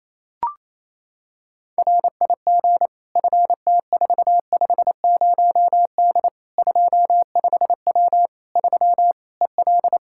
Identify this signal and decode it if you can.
Morse code, heard as a keyed tone, about 28 wpm, 700 Hz. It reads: RIG FT450D 25W 3 EL
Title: Practice QSO - Exchange 2 - Info - 28wpm